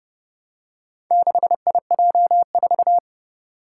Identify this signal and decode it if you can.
Morse code, heard as a keyed tone, about 30 wpm, 700 Hz. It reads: 6IJ4
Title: Four Challenging Letter-Number Combinations 30wpm